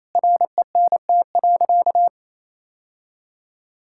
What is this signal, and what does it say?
Morse code, heard as a keyed tone, about 28 wpm, 700 Hz. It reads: RENT.
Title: Sentences from Top 1000 Words - Review Part 5 - 28wpm